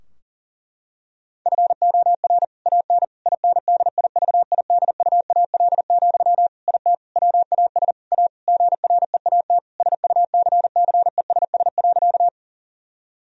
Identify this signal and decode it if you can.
Morse code, heard as a keyed tone, about 40 wpm, 700 Hz. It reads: FOR AN INDIVIDUAL, IT WAS A GREAT SUCCESS.
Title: Sentences from Top 1000 Words - Review Part 4 - 40wpm